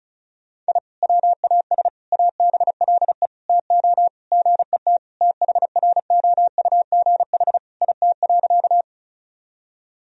Morse code, heard as a keyed tone, about 34 wpm, 700 Hz. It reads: I WAS ABLE TO GET THROUGH IT.
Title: Sentences from Top 1000 Words - Review Part 2 - 35wpm